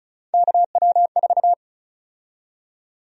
Morse code, heard as a keyed tone, about 36 wpm, 700 Hz. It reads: KW4
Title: USA Call Sign Prefixes - Repeated 2x - 35wpm